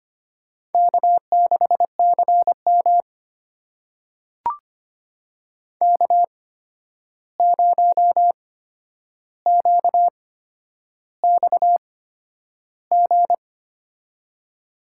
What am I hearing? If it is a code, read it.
Morse code, heard as a keyed tone, about 25 wpm, spaced as for 10 wpm, 700 Hz. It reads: K6CM K0QXG
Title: USA Call Signs - 8x Character Spacing - 25wpm